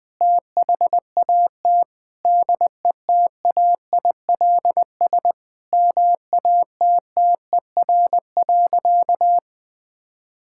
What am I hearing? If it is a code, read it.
Morse code, heard as a keyed tone, about 20 wpm, 700 Hz. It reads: THAT DETAILS MATTER.
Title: Sentences with 5-Word Phrases - 20wpm